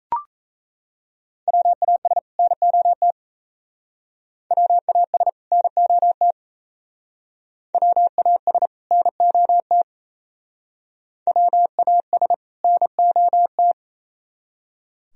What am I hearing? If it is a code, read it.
Morse code, heard as a keyed tone, about 34 wpm, 700 Hz. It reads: WAS NOT WAS NOT WAS NOT WAS NOT
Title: Speed-Racing - Sets of 2 Words 28wpm